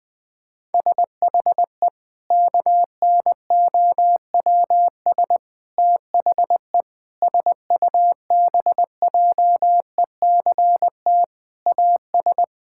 Morse code, heard as a keyed tone, about 20 wpm, 700 Hz. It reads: SHE KNOWS THE SUBJECT AS
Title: Sentences with 4-Word Phrases - 20wpm